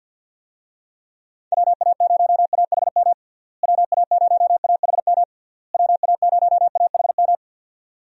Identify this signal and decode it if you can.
Morse code, heard as a keyed tone, about 52 wpm, 700 Hz. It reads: WA0AHM WA0AHM WA0AHM
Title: USA Call Signs - 6 Characters - Repeated 3x - 50wpm